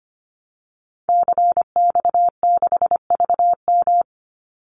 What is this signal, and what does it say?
Morse code, heard as a keyed tone, about 25 wpm, 700 Hz. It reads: CX6VM
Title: Call Signs Repeated 2x - 25wpm